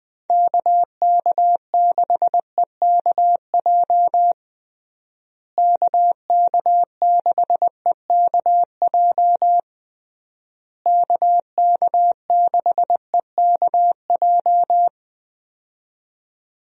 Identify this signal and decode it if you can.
Morse code, heard as a keyed tone, about 21 wpm, 700 Hz. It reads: KK6EKJ KK6EKJ KK6EKJ
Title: USA Call Signs - 6 Characters - Repeated 3x - 20wpm